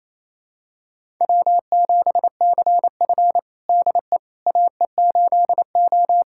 Morse code, heard as a keyed tone, about 27 wpm, 690 Hz. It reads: W7CF DE AE8O
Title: USA Callsign DE Callsign - 4 Characters - 28wpm